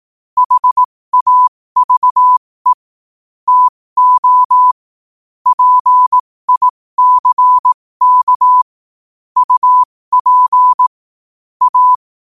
Morse code, heard as a keyed tone, about 18 wpm, spaced as for 15 wpm, 1000 Hz. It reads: HAVE TO PICK UP A
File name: qotdfiles / 15wpm